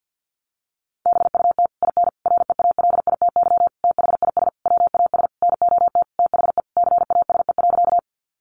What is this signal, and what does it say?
Morse code, heard as a keyed tone, about 50 wpm, 700 Hz. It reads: BUT IN REALITY THIS WAS NOT THE CASE.
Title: Sentences from Top 1200 Words 50wpm